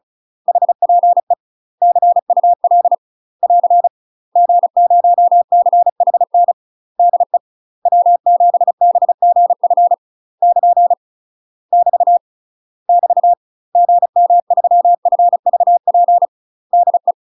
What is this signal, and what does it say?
Morse code, heard as a keyed tone, about 36 wpm, 700 Hz. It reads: HPE CUL <AR> G0CHN DE W7BGF <KN> <BT> <BT> GM3FVP DE
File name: UK MASTER QSOs TRAINING / 35WPM / UK MASTER QSOs 0013